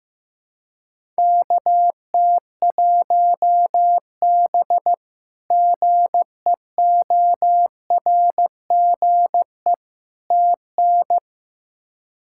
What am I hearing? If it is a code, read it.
Morse code, heard as a keyed tone, about 15 wpm, 700 Hz. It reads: KT1B GEORGE TN
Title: K1USN SST QSO Practice - 15wpm